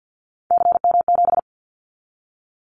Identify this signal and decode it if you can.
Morse code, heard as a keyed tone, about 50 wpm, 690 Hz. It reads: CM7
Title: Callsign Prefixes for Top 100 DXCC - 50wpm